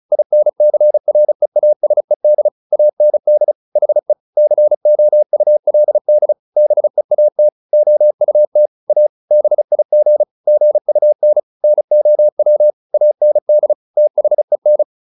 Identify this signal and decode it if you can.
Morse code, heard as a keyed tone, about 34 wpm, 600 Hz. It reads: INCREASED AND HE COULD BEAT OUT A BIG GUN NOW AND THEN